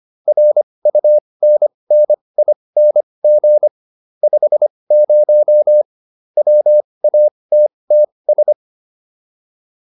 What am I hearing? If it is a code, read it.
Morse code, heard as a keyed tone, about 25 wpm, spaced as for 20 wpm, 600 Hz. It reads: RUNNING 50 WATTS